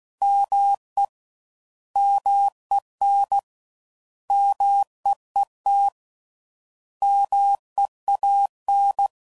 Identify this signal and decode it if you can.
Morse code, heard as a keyed tone, about 16 wpm, 800 Hz. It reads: ME MEN MEET MEAN